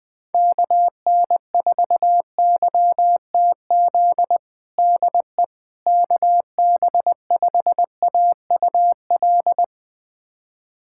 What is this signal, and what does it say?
Morse code, heard as a keyed tone, about 21 wpm, 700 Hz. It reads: KN4YTZ DE KB5AUL
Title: USA Callsign DE Callsign - 6 Characters - 20wpm